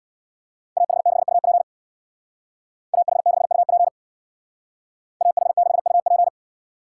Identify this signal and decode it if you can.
Morse code, heard as a keyed tone, about 58 wpm, 700 Hz. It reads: AH6UP AH6UP AH6UP
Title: Speed Racing - USA 2x2 Call Signs 50wpm